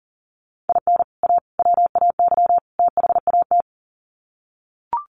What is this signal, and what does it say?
Morse code, heard as a keyed tone, about 40 wpm, 700 Hz. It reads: IN A WAY THAT
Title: Sets of 4 Words 40wpm